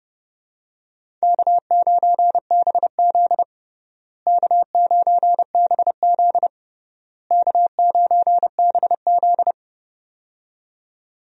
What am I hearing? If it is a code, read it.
Morse code, heard as a keyed tone, about 30 wpm, 700 Hz. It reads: K9BZ K9BZ K9BZ
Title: USA Call Signs - 4 Characters - Repeated 3x - 30wpm